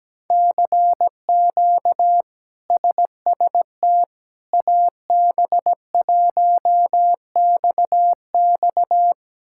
Morse code, heard as a keyed tone, about 17 wpm, 700 Hz. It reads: CQ SST AB1XX
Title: K1USN SST QSO Practice - 17wpm